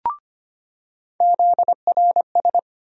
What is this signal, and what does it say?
Morse code, heard as a keyed tone, about 25 wpm, 700 Hz. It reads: ZRS
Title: USA Call Sign Postfixes - 3 Letters 25wpm